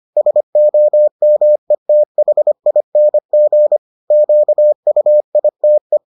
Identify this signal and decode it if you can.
Morse code, heard as a keyed tone, about 25 wpm, 600 Hz. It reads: SOMETHING QUITE